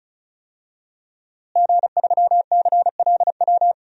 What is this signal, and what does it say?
Morse code, heard as a keyed tone, about 36 wpm, 700 Hz. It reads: G3CLW
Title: DXCC-100 Call Signs - 5 Characters - Repeated 3x - 35wpm